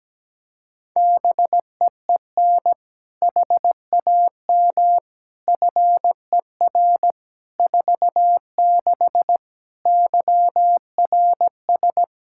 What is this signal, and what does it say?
Morse code, heard as a keyed tone, about 17 wpm, 700 Hz. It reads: BEEN HAM FER 46 YRS
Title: K1USN SST QSO Practice 17wpm